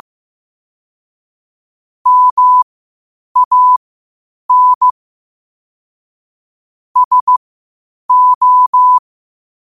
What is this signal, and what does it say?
Morse code, heard as a keyed tone, about 15 wpm, spaced as for 8 wpm, 1000 Hz. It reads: MAN SO
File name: qotdfiles / 8wpm